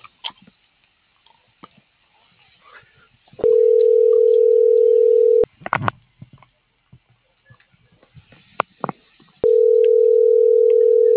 Ringback tone (North America), 440 + 480 Hz.